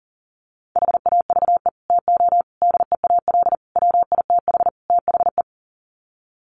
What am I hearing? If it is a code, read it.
Morse code, heard as a keyed tone, about 37 wpm, 700 Hz. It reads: HAVE TO DEAL WITH THE
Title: Sets of 5 Words Encore2 40wpm